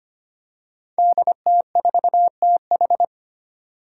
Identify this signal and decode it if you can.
Morse code, heard as a keyed tone, about 25 wpm, 700 Hz. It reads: DT4TH
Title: Five Challenging Letter-Number Combinations 25wpm